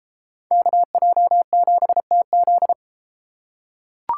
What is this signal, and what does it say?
Morse code, heard as a keyed tone, about 31 wpm, 700 Hz. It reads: KJ7TZ